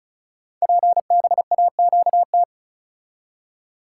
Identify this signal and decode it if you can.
Morse code, heard as a keyed tone, about 36 wpm, 690 Hz. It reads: PBAQT
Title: Code Groups - 5 Letters - 35wpm